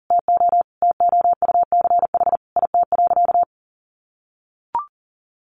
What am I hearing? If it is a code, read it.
Morse code, heard as a keyed tone, about 40 wpm, 700 Hz. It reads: TO TOUCH IT.
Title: Sentences from Top 1200 Words 40wpm